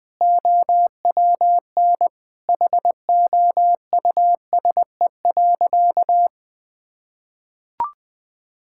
Morse code, heard as a keyed tone, about 20 wpm, 700 Hz. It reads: OWN HOUSE.
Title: Sentences with 2-Word Phrases - Spoken Beginning - 20wpm